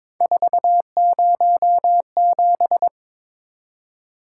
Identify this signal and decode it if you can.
Morse code, heard as a keyed tone, about 22 wpm, 700 Hz. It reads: 407
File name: practice / 22wpm / Three Letter-Number Combinations - 22wpm